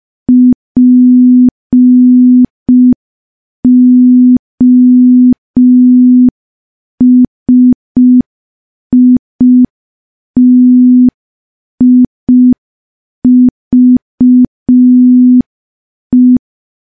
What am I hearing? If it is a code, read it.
Morse code, heard as a keyed tone, about 5 wpm, 250 Hz. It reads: POSITIVE
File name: morse / morse-training-05WPM